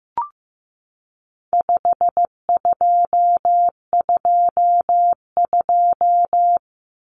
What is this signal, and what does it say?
Morse code, heard as a keyed tone, about 15 wpm, 700 Hz. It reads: 5222
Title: Numbers - 4-Digit 15wpm